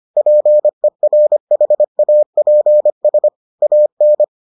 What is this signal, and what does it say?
Morse code, heard as a keyed tone, about 25 wpm, 600 Hz. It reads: PERHAPS AN